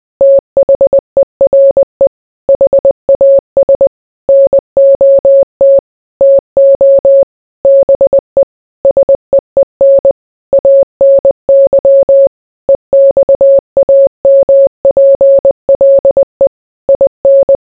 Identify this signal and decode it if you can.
Morse code, heard as a keyed tone, about 20 wpm, 550 Hz. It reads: THERE HAS NOT TO BE SEEN ANY EXAMPLE IN